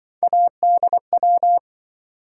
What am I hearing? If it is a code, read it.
Morse code, heard as a keyed tone, about 27 wpm, 700 Hz. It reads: ADW